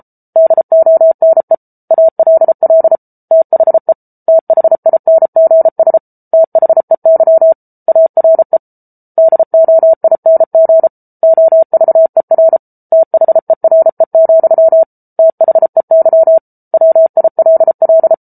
Morse code, heard as a keyed tone, about 34 wpm, 650 Hz. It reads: DONE ALL THE THINGS THEY ARE DOING OVER THERE, THEY WILL